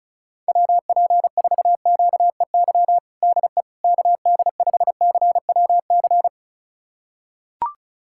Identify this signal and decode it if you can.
Morse code, heard as a keyed tone, about 36 wpm, 700 Hz. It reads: WP4QEY DE KD5CWC
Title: USA Callsign DE Callsign 35wpm